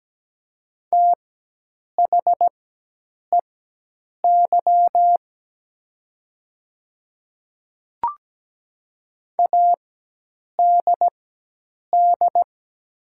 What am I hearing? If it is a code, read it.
Morse code, heard as a keyed tone, about 17 wpm, spaced as for 8 wpm, 700 Hz. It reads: THEY ADD
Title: 4x Character Spacing - Top 500 Words - 17wpm